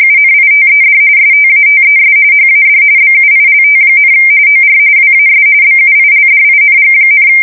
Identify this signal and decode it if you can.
RTTY, about 45.45 baud. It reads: UNITED STATES HAD BUILT, OR WAS BUILDING,
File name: fsk170 / battleshipnj